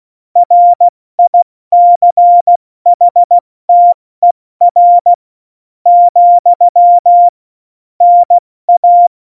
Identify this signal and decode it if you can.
Morse code, heard as a keyed tone, about 16 wpm, spaced as for 13 wpm, 700 Hz. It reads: RICHTER, NA
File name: Morsekurs / Witze / Witz 05. Januar 2021, 70 F80